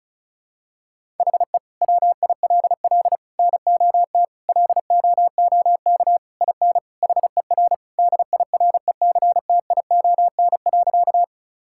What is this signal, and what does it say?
Morse code, heard as a keyed tone, about 36 wpm, 700 Hz. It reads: HE WILL NOT LOOK IN HER DIRECTION.
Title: Sentences from Top 1000 Words - Review Part 5 - 35wpm